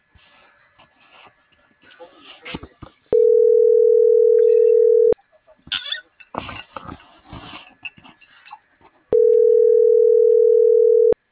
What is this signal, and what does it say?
Ringback tone (North America), 440 + 480 Hz.